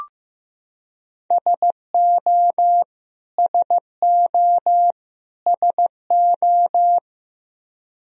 Morse code, heard as a keyed tone, about 15 wpm, 700 Hz. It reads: SO SO SO
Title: Top 100 Words Repeated 15wpm